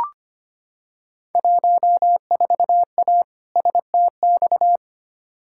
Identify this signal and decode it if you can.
Morse code, heard as a keyed tone, about 25 wpm, 700 Hz. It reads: 14A STX
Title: Field Day Exchanges Encore - 25wpm